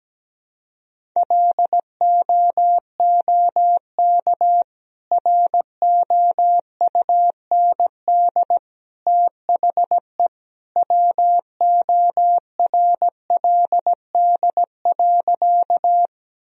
Morse code, heard as a keyed tone, about 17 wpm, 700 Hz. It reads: LOOK ROUND THE WORLD.
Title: Sentences from Top 1000 Words - Review Part 1 - 17wpm